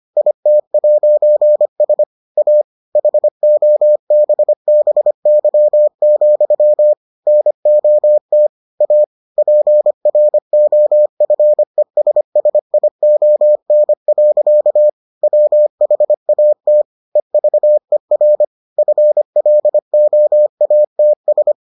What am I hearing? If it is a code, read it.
Morse code, heard as a keyed tone, about 25 wpm, 600 Hz. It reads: IT'S A HOBBY, NOT A PROFESSION. WHAT EVER FLOATS